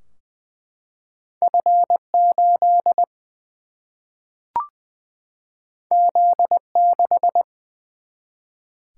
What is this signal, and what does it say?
Morse code, heard as a keyed tone, about 20 wpm, 700 Hz. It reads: F 8 Z 6
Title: Single-Letter Single-Number 20wpm